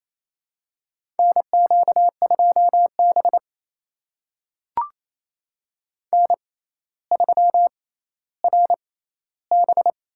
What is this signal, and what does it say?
Morse code, heard as a keyed tone, about 28 wpm, spaced as for 14 wpm, 700 Hz. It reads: NQ2B N3RB
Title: USA Call Signs - 4 Characters - 6x Character Spacing - 28wpm